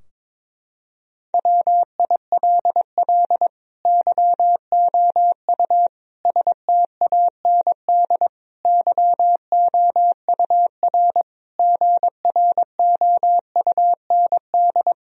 Morse code, heard as a keyed tone, about 22 wpm, 700 Hz. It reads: WILL YOU STAND YOUR GROUND
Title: Sentences from Top 1000 Words - Review Part 2 - 22wpm